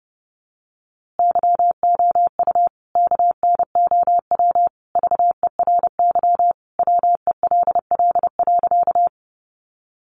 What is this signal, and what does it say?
Morse code, heard as a keyed tone, about 29 wpm, 700 Hz. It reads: YOU KNOW VERY WELL.
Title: A Sound Of Thunder 30wpm